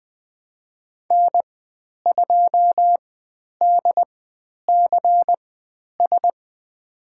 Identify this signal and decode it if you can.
Morse code, heard as a keyed tone, about 20 wpm, spaced as for 10 wpm, 700 Hz. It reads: N2DCS
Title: 109 Lesson 18g - Letter m - Call Signs 20wpm